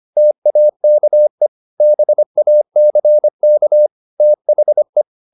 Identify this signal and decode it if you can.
Morse code, heard as a keyed tone, about 25 wpm, 600 Hz. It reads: TAKE BACK THE
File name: the old man and the sea part 1 25wpm